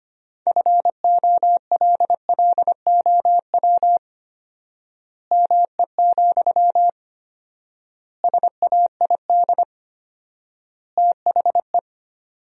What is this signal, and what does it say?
Morse code, heard as a keyed tone, about 25 wpm, 700 Hz. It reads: FOLLOW ME, SAID THE
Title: 4x Word-Spacing - Sentences from Top 200 Words - 25wpm